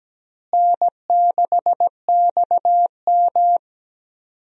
Morse code, heard as a keyed tone, about 17 wpm, 700 Hz. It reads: N6XM
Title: USA Call Signs - 4 Characters - Repeated 2x - 17wpm